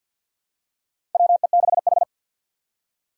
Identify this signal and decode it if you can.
Morse code, heard as a keyed tone, about 52 wpm, 700 Hz. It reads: WE6H
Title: USA Call Signs - 3x Character Spacing - 50wpm